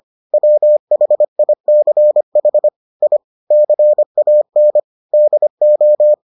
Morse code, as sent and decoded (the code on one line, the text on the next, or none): .-- .... .. -.-. .... / .. / -.-. .- -. / -.. ---
WHICH I CAN DO